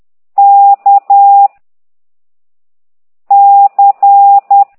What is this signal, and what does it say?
Morse code, heard as a keyed tone, about 10 wpm, 800 Hz. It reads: K C
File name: exerc7